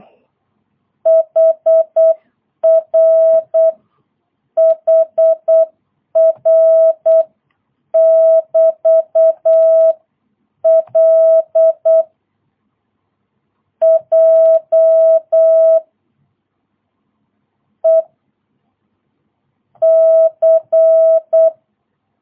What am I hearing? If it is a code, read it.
Morse code, heard as a keyed tone, about 8 wpm, spaced as for 7 wpm, 650 Hz. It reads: HRHR<BT>L J E C
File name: jh8cbh / 2022cw